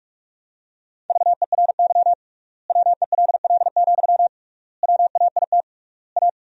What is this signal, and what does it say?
Morse code, heard as a keyed tone, about 41 wpm, 700 Hz. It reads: VERY WELL, WAIT A
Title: 3x Word-Spacing - Sentences from Top 500 Words - 45wpm